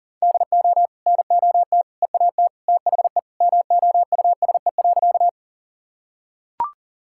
Morse code, heard as a keyed tone, about 37 wpm, 700 Hz. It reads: DO NOT EAT THE MOUSE.
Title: Sentences from Top 1000 Words - Review Part 5 - 40wpm